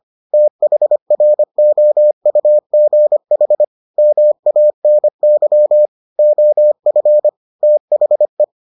Morse code, heard as a keyed tone, about 25 wpm, 600 Hz. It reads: THROUGH MANY OF THE